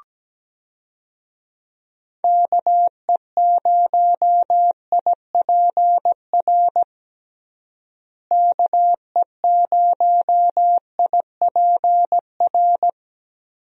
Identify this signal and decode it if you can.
Morse code, heard as a keyed tone, about 17 wpm, 700 Hz. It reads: KE0IPR KE0IPR
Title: USA Call Signs - 6 Characters - Repeated 3x - 17wpm